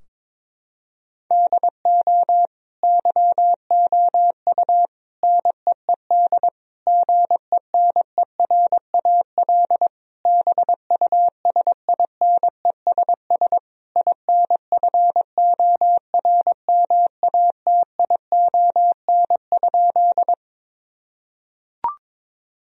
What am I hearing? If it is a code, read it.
Morse code, heard as a keyed tone, about 22 wpm, 700 Hz. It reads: DO YOU NEED GENERAL BUSINESS INFORMATION?
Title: Sentences from Top 1000 Words - Review Part 3 - 22wpm